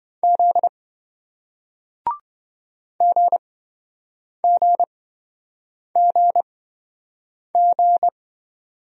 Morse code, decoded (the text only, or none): Z GGGG